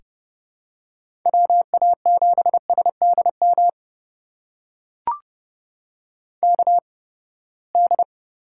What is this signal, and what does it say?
Morse code, heard as a keyed tone, about 30 wpm, spaced as for 21 wpm, 700 Hz. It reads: WA7SDM KD
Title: USA Call Signs - 6 Characters - 8x Character Spacing - 30wpm